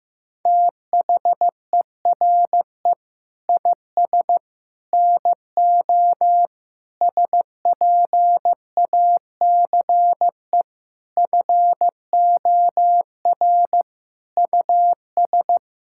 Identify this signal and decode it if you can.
Morse code, heard as a keyed tone, about 15 wpm, 700 Hz. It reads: THERE IS NO SPACE FOR US
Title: Sentences from Top 1000 Words - Review Part 5 - 15wpm